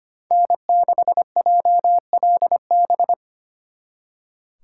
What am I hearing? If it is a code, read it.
Morse code, heard as a keyed tone, about 33 wpm, 700 Hz. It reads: N6JLB